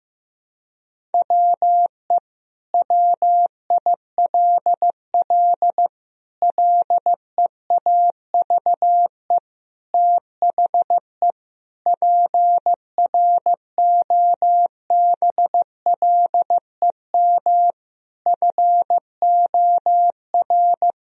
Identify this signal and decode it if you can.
Morse code, heard as a keyed tone, about 16 wpm, 700 Hz. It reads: WE WILL LEAVE THE PROBLEM FOR
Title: Sentences from Top 1000 Words - Review Part 2 - 15wpm